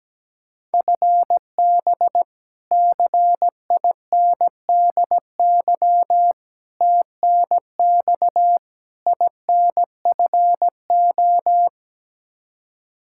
Morse code, heard as a keyed tone, about 17 wpm, 700 Hz. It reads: FB CINDY TNX INFO
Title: Practice QSO - Exchange 3 - More Info - 17wpm